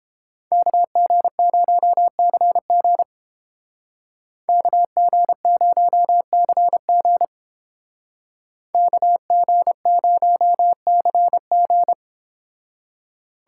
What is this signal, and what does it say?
Morse code, heard as a keyed tone, about 29 wpm, 700 Hz. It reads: KG0CG KG0CG KG0CG